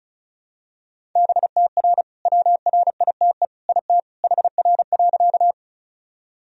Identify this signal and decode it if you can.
Morse code, heard as a keyed tone, about 34 wpm, 700 Hz. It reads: BTR WRITE IT HR.
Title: Sentences from Top 400 Words and Abbreviations 35wpm